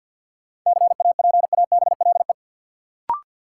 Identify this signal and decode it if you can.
Morse code, heard as a keyed tone, about 51 wpm, 700 Hz. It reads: CAPABLE
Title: Sentences from Top 1200 Words - New Word Drill Alt - 50wpm